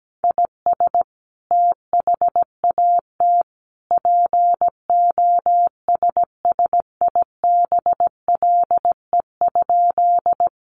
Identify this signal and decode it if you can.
Morse code, heard as a keyed tone, about 17 wpm, 700 Hz. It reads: IS THAT POSSIBLE?
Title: Sentences from Top 500 Words 17wpm